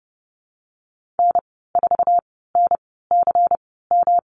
Morse code, heard as a keyed tone, about 30 wpm, spaced as for 17 wpm, 700 Hz. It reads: N4NCM
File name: Lesson 19g - Letter w - Call Signs 30wpm